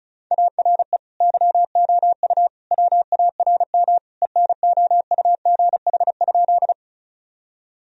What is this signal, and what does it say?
Morse code, heard as a keyed tone, about 36 wpm, 700 Hz. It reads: ARE YOU WARM ENOUGH?
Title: Sentences from Top 1000 Words - Review Part 2 - 35wpm